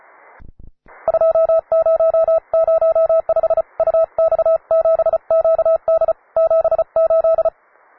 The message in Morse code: 2005 UX7QD 78